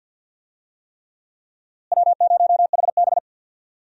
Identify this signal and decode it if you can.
Morse code, heard as a keyed tone, about 48 wpm, 700 Hz. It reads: W0HB
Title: USA Call Signs - 4 Characters - Repeated 3x - 50wpm